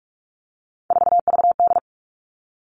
Morse code, heard as a keyed tone, about 44 wpm, 700 Hz. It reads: 4VD
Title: All Cut Numbers - 3-Digit 45wpm